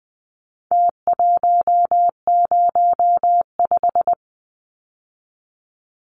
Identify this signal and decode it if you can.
Morse code, heard as a keyed tone, about 20 wpm, 700 Hz. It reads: T105
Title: Numbers - Contest Style - 4-Digit 20wpm